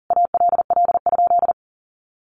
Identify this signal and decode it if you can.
Morse code, heard as a keyed tone, about 40 wpm, 700 Hz. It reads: ALL?
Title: Sentences from Top 800 Words 40wpm